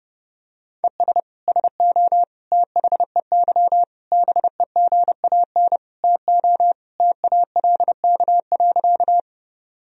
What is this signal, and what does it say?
Morse code, heard as a keyed tone, about 30 wpm, 700 Hz. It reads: ES SO THEY BEGAN TO TALK.
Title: Sentences from Top 400 Words and Abbreviations 30wpm